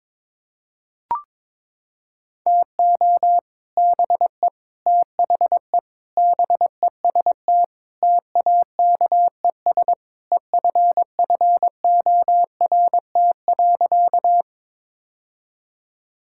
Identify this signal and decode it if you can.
Morse code, heard as a keyed tone, about 22 wpm, 700 Hz. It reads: TO BE THE BEST TAKES EFFORT.
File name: practice / 22wpm / Sentences with 3-Word Phrases - 22wpm